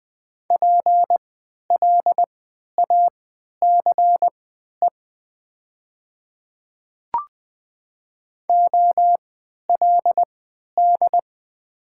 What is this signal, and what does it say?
Morse code, heard as a keyed tone, about 20 wpm, spaced as for 11 wpm, 700 Hz. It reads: PLACE OLD
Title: 3x Character Spacing - Top 200 Words - 20wpm